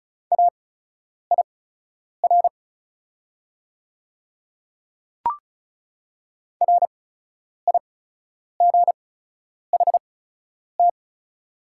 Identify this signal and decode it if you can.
Morse code, heard as a keyed tone, about 36 wpm, spaced as for 10 wpm, 690 Hz. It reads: AIR RIGHT
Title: 8x Character Spacing - Top 200 Words - 35wpm